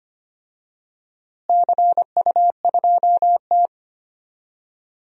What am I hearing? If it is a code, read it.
Morse code, heard as a keyed tone, about 25 wpm, 700 Hz. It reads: CU2T
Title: DXCC-100 Call Signs - 4 Characters - Repeated 3x - 25wpm